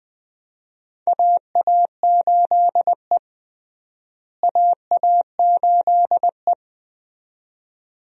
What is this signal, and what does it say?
Morse code, heard as a keyed tone, about 20 wpm, 700 Hz. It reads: AA8E AA8E
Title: USA Call Signs - 4 Characters - Repeated 2x - 20wpm